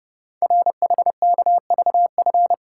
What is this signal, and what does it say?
Morse code, heard as a keyed tone, about 30 wpm, 700 Hz. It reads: RHKVF
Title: Code Groups - 5 Letters - 30wpm